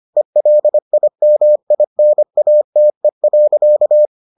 Morse code, heard as a keyed tone, about 25 wpm, 600 Hz. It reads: ELIMINATE.